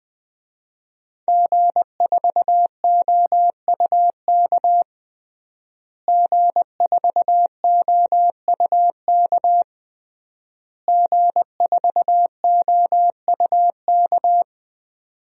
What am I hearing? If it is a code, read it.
Morse code, heard as a keyed tone, about 20 wpm, 700 Hz. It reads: G4OUK G4OUK G4OUK
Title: DXCC-100 Call Signs - 5 Characters - Repeated 3x - 20wpm